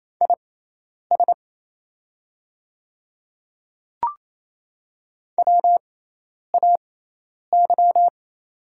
Morse code, heard as a keyed tone, about 28 wpm, spaced as for 10 wpm, 700 Hz. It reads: IS WAY
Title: 6x Character Spacing - Top 100 Words - 28wpm